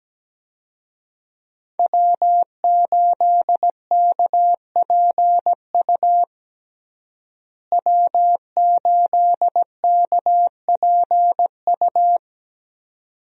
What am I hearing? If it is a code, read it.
Morse code, heard as a keyed tone, about 17 wpm, 700 Hz. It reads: W8KPU W8KPU
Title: USA Call Signs - 5 Characters - Repeated 2x - 17wpm